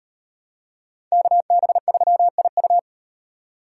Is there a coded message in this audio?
Yes: Morse code from 1.1 to 2.8 s, heard as a keyed tone, about 38 wpm, 700 Hz. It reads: KB3IU